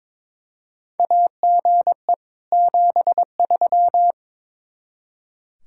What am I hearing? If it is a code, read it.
Morse code, heard as a keyed tone, about 22 wpm, 700 Hz. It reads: AGE 73
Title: Practice QSO - Exchange 3 - More Info - 22wpm